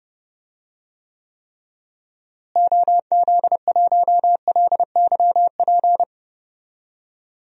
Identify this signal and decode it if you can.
Morse code, heard as a keyed tone, about 30 wpm, 700 Hz. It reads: OZ1LYP
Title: DXCC-100 Call Signs - 6 Characters - Repeated 3x - 30wpm